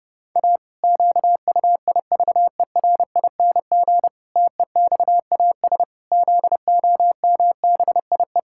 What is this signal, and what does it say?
Morse code, heard as a keyed tone, about 32 wpm, 700 Hz. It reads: A QUIVERING TEXAS ZOMBIE
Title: Pangram Challenge v2 - 30wpm